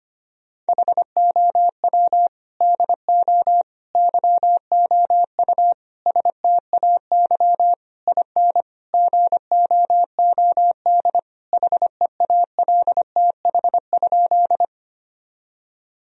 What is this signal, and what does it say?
Morse code, heard as a keyed tone, about 25 wpm, 700 Hz. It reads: HOW DO YOU STAY IN GOOD HEALTH?
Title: Sentences from Top 1000 Words - Review Part 3 - 25wpm